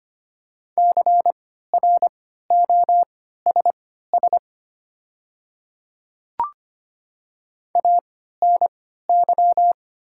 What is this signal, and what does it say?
Morse code, heard as a keyed tone, about 25 wpm, spaced as for 14 wpm, 700 Hz. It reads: CROSS ANY
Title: 3x Character Spacing - Top 300 Words - 25wpm